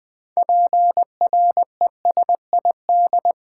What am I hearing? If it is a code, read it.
Morse code, heard as a keyed tone, about 20 wpm, 700 Hz. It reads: PRESID
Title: Build A Word - 6 Letters At A Time - 12-Letter Words - 20wpm